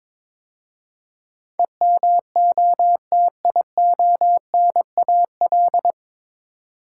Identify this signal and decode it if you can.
Morse code, heard as a keyed tone, about 22 wpm, 700 Hz. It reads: EMOTIONAL
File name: 9 Letter Words - First 3 Letters Given - 22wpm